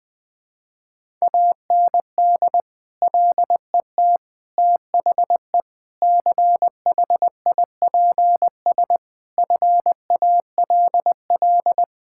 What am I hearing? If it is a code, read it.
Morse code, heard as a keyed tone, about 21 wpm, 700 Hz. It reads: AND LET THE CHIPS FALL
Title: Sentences with 8-Word Phrases - 20wpm